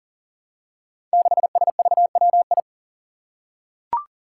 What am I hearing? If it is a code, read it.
Morse code, heard as a keyed tone, about 40 wpm, 700 Hz. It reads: 6SVWI
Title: Five Challenging Letter-Number Combinations 40wpm